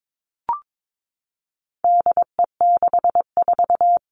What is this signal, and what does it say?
Morse code, heard as a keyed tone, about 22 wpm, 700 Hz. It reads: DE64
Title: All Cut Numbers - 4-Digit 22wpm